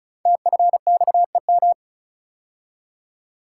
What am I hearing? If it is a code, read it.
Morse code, heard as a keyed tone, about 36 wpm, 690 Hz. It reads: TFXEM
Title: Code Groups - 5 Letters - 35wpm